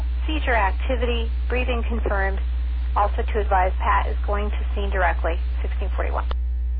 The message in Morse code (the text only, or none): none